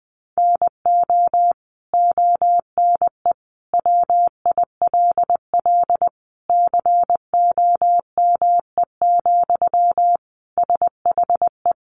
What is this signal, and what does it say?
Morse code, heard as a keyed tone, about 20 wpm, 700 Hz. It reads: NO ONE WILL COME, SHE
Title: Sentences from Top 100 Words 20wpm